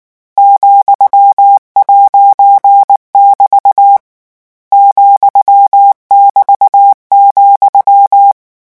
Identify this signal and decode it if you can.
Morse code, heard as a keyed tone, about 19 wpm, 800 Hz. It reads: ,'<BT> ,<BT>,
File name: morse mp3 / lecon14-vitesse 18